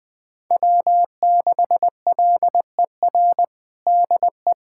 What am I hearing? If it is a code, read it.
Morse code, heard as a keyed tone, about 20 wpm, 700 Hz. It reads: W6LER DE
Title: USA Callsign DE Callsign 20wpm